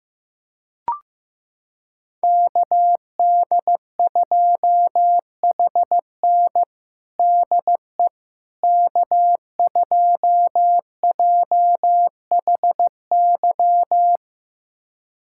Morse code, heard as a keyed tone, about 16 wpm, 700 Hz. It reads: KD2HN DE K2JHY
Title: USA Callsign DE Callsign - 5 Characters - 15wpm